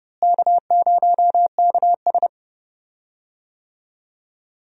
Morse code, heard as a keyed tone, about 30 wpm, 700 Hz. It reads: K0KS
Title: USA Call Signs - 4 Characters - Repeated 2x - 30wpm